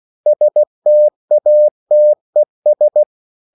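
Morse code, heard as a keyed tone, about 16 wpm, 600 Hz. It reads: STATES
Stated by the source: CW practice of national capitals at 16 words per minute, from ON6ZQ.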